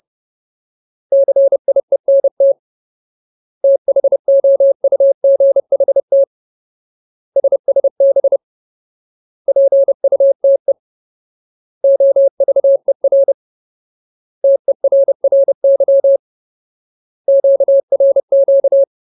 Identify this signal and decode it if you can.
Morse code, heard as a keyed tone, about 30 wpm, 550 Hz. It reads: CIENT THOUGHT SSB PUTE OVER TERRY QRQ
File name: Ultimate Word List / 30WPM Ultimate